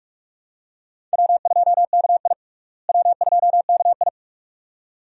Morse code, heard as a keyed tone, about 42 wpm, 700 Hz. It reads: W2KI W2KI
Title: USA Call Signs - 4 Characters - Repeated 2x - 45wpm